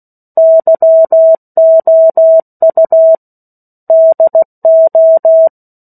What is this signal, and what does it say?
Morse code, heard as a keyed tone, about 16 wpm, 650 Hz. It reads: YOU DO
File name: 03-Chapter-03 16 WPM